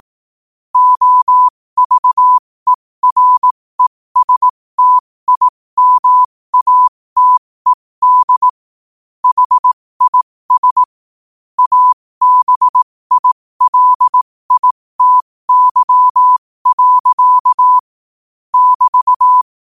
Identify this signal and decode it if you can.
Morse code, heard as a keyed tone, about 18 wpm, spaced as for 15 wpm, 1000 Hz. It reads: OVERESTIMATED HIS ABILITY. <BT>
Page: New quotes every day in morse code at 15 Words per minute.